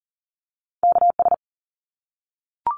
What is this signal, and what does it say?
Morse code, heard as a keyed tone, about 40 wpm, 700 Hz. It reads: KS
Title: ARRL Sections 40wpm